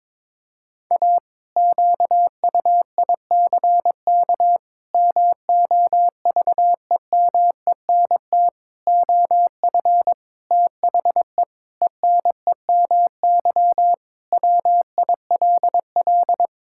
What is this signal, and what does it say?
Morse code, heard as a keyed tone, about 22 wpm, 700 Hz. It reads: A QUICK MOVEMENT OF THE ENEMY WILL
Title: Pangram Challenge v2 - 22wpm